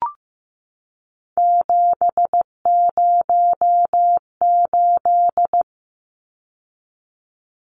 Morse code, heard as a keyed tone, about 15 wpm, 700 Hz. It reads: 708
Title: Numbers - 3-Digit 15wpm